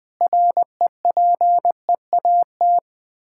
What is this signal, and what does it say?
Morse code, heard as a keyed tone, about 20 wpm, 700 Hz. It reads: REPEAT
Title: Sentences with 8-Word Phrases - 20wpm